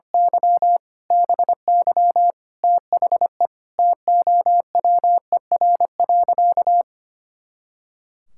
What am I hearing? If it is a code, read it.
Morse code, heard as a keyed tone, about 25 wpm, 700 Hz. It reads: Y BY THE TOWER.
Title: Pangram Challenge v2 - 25wpm